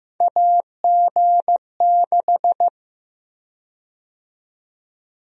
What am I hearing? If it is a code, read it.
Morse code, heard as a keyed tone, about 15 wpm, 700 Hz. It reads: AG6
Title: USA Call Sign Prefixes - Repeated 2x - 15wpm